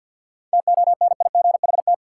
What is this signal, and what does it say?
Morse code, heard as a keyed tone, about 52 wpm, 700 Hz. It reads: TONIGHT
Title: Sentences from Top 600 Words - New Word Drill - 50wpm